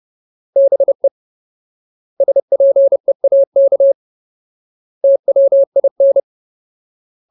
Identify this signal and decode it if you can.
Morse code, heard as a keyed tone, about 30 wpm, 550 Hz. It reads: BE SPEAK TWIN
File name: Ultimate Word List / 30WPM Ultimate